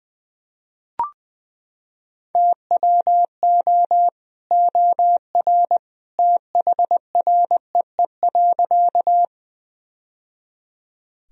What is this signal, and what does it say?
Morse code, heard as a keyed tone, about 20 wpm, 700 Hz. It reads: TWO OR THREE.
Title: Sentences from Top 1000 Words - Review Part 1 - 20wpm